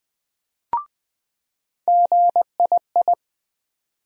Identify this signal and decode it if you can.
Morse code, heard as a keyed tone, about 20 wpm, 700 Hz. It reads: GII
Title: USA Call Sign Postfixes - 3 Letters 20wpm